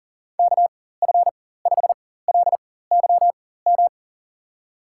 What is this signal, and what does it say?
Morse code, heard as a keyed tone, about 40 wpm, spaced as for 19 wpm, 700 Hz. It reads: KF5LYM
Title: USA Call Signs - 6 Characters - 4x Character Spacing - 40wpm